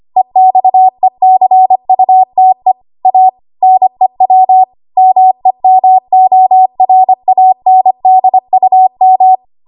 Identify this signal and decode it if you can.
Morse code, heard as a keyed tone, about 25 wpm, 750 Hz. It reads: EXECUTE A NEW MEMORANDUM
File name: Archive / 25 WPM / 25WPM